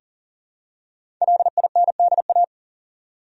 Morse code, heard as a keyed tone, about 40 wpm, 700 Hz. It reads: LINDA
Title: USA Common Names - Max 5 Letters - 40wpm